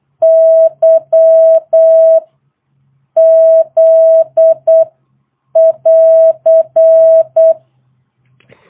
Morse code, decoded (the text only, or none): Y Z <AR>